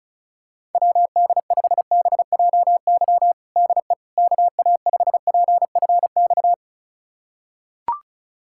Morse code, heard as a keyed tone, about 34 wpm, 700 Hz. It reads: WD5BJY DE KA5PFX
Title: USA Callsign DE Callsign - 6 Characters - 35wpm